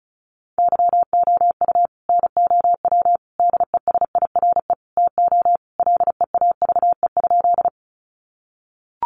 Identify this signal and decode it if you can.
Morse code, heard as a keyed tone, about 34 wpm, 700 Hz. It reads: YOU NOW DESIRE TO LEAVE?
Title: Sentences from Top 1100 Words 35wpm